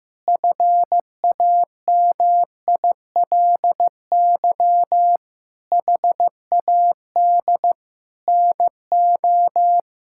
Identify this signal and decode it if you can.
Morse code, heard as a keyed tone, about 15 wpm, 700 Hz. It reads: FAMILY HAD NO
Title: Sentences from Top 1600 Words 15wpm